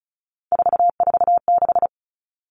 Morse code, heard as a keyed tone, about 34 wpm, 700 Hz. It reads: 446
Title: Numbers - 3-Digit 35wpm